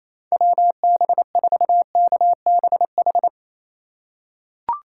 Morse code, heard as a keyed tone, about 28 wpm, 700 Hz. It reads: WB4KBH
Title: USA Call Signs - 6 Characters - 2x Character Spacing - 28wpm